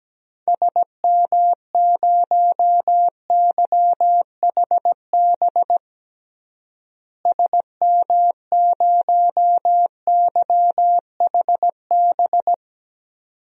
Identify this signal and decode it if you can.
Morse code, heard as a keyed tone, about 17 wpm, 700 Hz. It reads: SM0YHB SM0YHB
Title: DXCC-100 Call Signs - 6 Characters - Repeated 2x - 17wpm